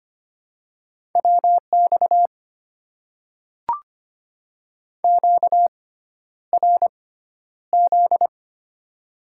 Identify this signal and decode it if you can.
Morse code, heard as a keyed tone, about 25 wpm, spaced as for 11 wpm, 700 Hz. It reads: WX QRZ
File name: Top 90 QSO Elements - 6x Character Spacing - 25wpm